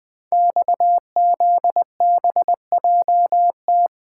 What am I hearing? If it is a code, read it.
Morse code, heard as a keyed tone, about 20 wpm, 700 Hz. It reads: XZBJT
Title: Code Groups - 5 Letter:Numbers - 20wpm